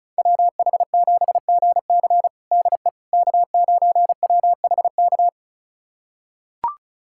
Morse code, heard as a keyed tone, about 36 wpm, 700 Hz. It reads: WH7GC DE K9WHK
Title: USA Callsign DE Callsign - 5 Characters - 35wpm